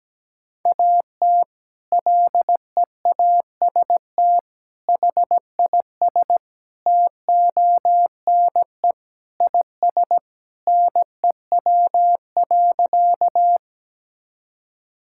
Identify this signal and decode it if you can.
Morse code, heard as a keyed tone, about 17 wpm, 700 Hz. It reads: AT LEAST HIS TONE IS NEW.
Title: Sentences from Top 1600 Words 17wpm